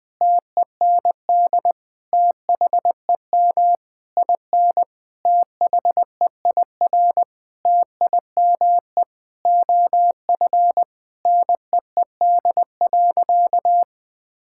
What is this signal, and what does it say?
Morse code, heard as a keyed tone, about 20 wpm, 700 Hz. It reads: TEND THEM IN THEIR TIME OF NEED.
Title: Sentences from Top 1000 Words - Review Part 5 - 20wpm